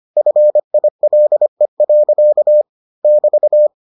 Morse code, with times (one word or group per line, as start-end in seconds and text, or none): 0.2-2.6 FILE.
3.0-3.7 <BT>